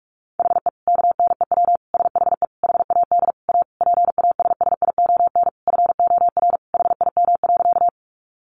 Morse code, heard as a keyed tone, about 44 wpm, 700 Hz. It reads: HE KNEW SHE HAD A PASSION FOR HIM.
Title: Sentences from Top 1300 Words 45wpm